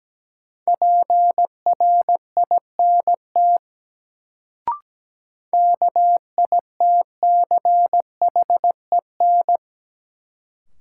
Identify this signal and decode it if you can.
Morse code, heard as a keyed tone, about 17 wpm, 700 Hz. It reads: PRINT KITCHEN
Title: Sentences from Top 1400 Words - New Word Drill Alt - 17wpm